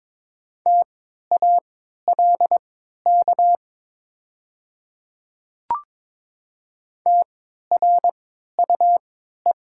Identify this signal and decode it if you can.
Morse code, heard as a keyed tone, about 22 wpm, spaced as for 12 wpm, 700 Hz. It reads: TALK TRUE
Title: 3x Character Spacing - Top 500 Words - 22wpm